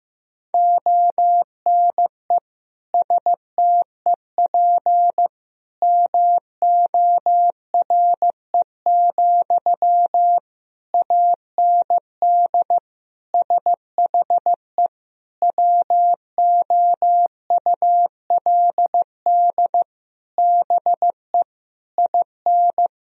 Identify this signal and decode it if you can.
Morse code, heard as a keyed tone, about 16 wpm, 700 Hz. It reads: ONE STEP MORE, AND SHE WOULD BE IN
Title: Sentences from Top 1000 Words - Review Part 3 - 15wpm